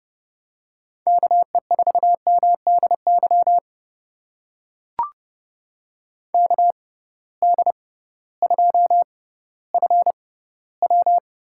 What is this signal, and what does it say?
Morse code, heard as a keyed tone, about 30 wpm, spaced as for 16 wpm, 700 Hz. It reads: KE4MDY KD2FW
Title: USA Call Signs - 6 Characters - 6x Character Spacing - 30wpm